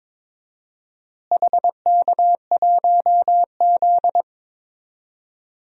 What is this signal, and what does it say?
Morse code, heard as a keyed tone, about 22 wpm, 700 Hz. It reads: HK1Z
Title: DXCC-100 Call Signs - 4 Characters - Repeated 3x - 22wpm